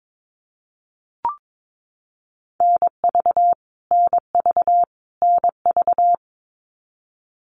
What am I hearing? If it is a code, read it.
Morse code, heard as a keyed tone, about 22 wpm, 700 Hz. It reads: NV NV NV
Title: US States Repeated 3x 22wpm